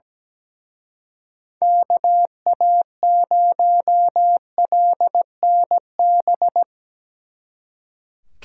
Morse code, heard as a keyed tone, about 19 wpm, 700 Hz. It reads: KA0LNB
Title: Speed Racing - USA Call Signs 17wpm